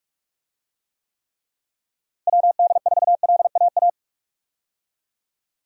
Morse code, heard as a keyed tone, about 42 wpm, 700 Hz. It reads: WD4LAA
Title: USA Call Signs - 6 Characters - Repeated 3x - 45wpm